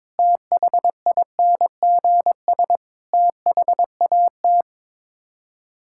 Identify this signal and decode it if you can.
Morse code, heard as a keyed tone, about 22 wpm, 700 Hz. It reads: THINGS THAT
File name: practice / 22wpm / Build A Phrase - 5-Word Phrases - 22wpm